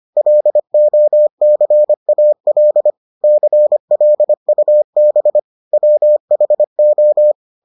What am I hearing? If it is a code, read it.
Morse code, heard as a keyed tone, about 25 wpm, 600 Hz. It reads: LOCAL CLUB WHO